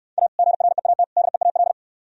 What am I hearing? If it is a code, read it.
Morse code, heard as a keyed tone, about 66 wpm, 700 Hz. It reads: A GREAT DEAL